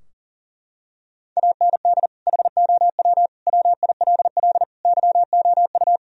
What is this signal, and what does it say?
Morse code, heard as a keyed tone, about 40 wpm, 700 Hz. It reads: AND HOW WILL YOU
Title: Sentences from Top 1000 Words - Review Part 4 - 40wpm